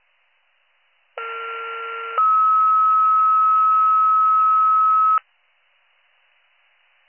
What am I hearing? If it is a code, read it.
Two-tone page, heard as a one-second tone, then a three-second tone, about 1370 then 1290 Hz. It is heard from 1.2 to 5.2 s.